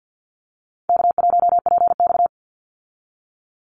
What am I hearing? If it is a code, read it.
Morse code, heard as a keyed tone, about 49 wpm, 700 Hz. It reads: K1PX
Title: Easy Call Signs Repeated 3x - 50wpm